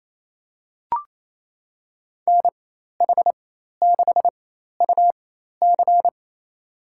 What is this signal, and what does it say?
Morse code, heard as a keyed tone, about 28 wpm, spaced as for 13 wpm, 700 Hz. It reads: NH6UC
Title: USA Call Signs - 5 Characters - 4x Character Spacing - 28wpm